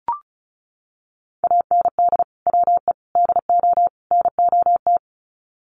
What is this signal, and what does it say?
Morse code, heard as a keyed tone, about 34 wpm, 700 Hz. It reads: AND WE DO NOT
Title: Sets of 4 Words 35wpm